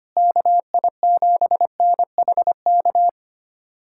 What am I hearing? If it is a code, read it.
Morse code, heard as a keyed tone, about 25 wpm, 700 Hz. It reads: KI7NHK
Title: USA Callsign DE Callsign 25wpm